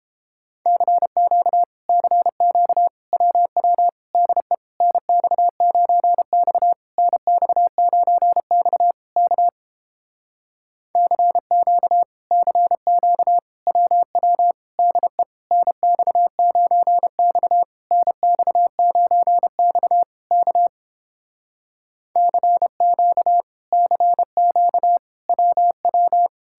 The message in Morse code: CQ CQ WW DE NX9X NX9X K CQ CQ WW DE NX9X NX9X K CQ CQ WW

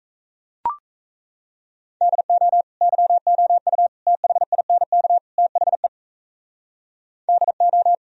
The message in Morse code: DO YOU THINK THE DO

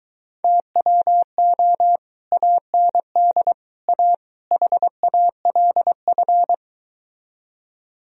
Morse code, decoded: TWO AND A HALF

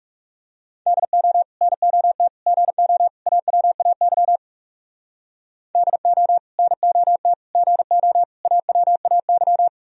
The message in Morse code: DO NOT GO AWAY DO NOT GO AWAY

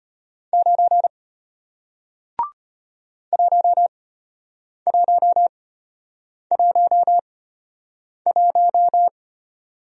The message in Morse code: 9 1111

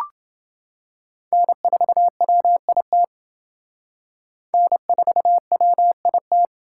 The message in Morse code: N4WIT N4WIT

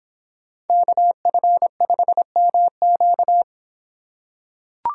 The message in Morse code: KF5MQ

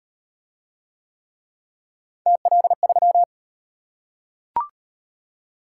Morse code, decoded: TL3

